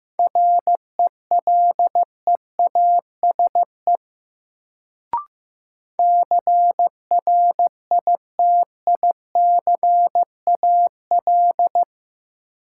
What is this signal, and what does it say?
Morse code, heard as a keyed tone, about 15 wpm, 700 Hz. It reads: RELEASE CRITICAL
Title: Sentences from Top 1100 Words - New Word Drill Alt - 15wpm